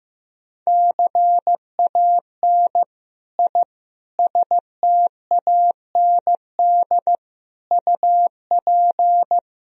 Morse code, heard as a keyed tone, about 15 wpm, 700 Hz. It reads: CAN I STAND UP
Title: Sentences from Top 1000 Words - Review Part 1 - 15wpm